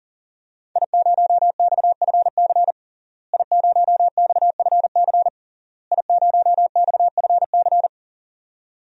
Morse code, heard as a keyed tone, about 40 wpm, 700 Hz. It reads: I0XFC I0XFC I0XFC
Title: DXCC-100 Call Signs - 5 Characters - Repeated 3x - 40wpm